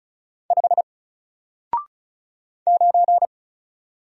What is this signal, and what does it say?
Morse code, heard as a keyed tone, about 35 wpm, 700 Hz. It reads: 5 9
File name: Single-Letter-Number-35wpm